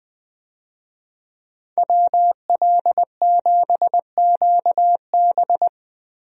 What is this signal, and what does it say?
Morse code, heard as a keyed tone, about 20 wpm, 700 Hz. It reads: WL7QB
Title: USA Call Signs - 5 Characters - Repeated 2x - 20wpm